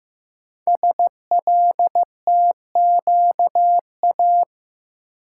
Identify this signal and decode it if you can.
Morse code, heard as a keyed tone, about 15 wpm, 700 Hz. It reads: SLTQA
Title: Code Groups - 5 Letters - 15wpm